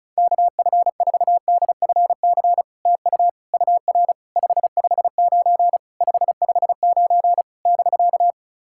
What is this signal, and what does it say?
Morse code, heard as a keyed tone, about 36 wpm, 690 Hz. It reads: KF4DFC TU UR 559 559 <BK>
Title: SOTA QSO Practice 35wpm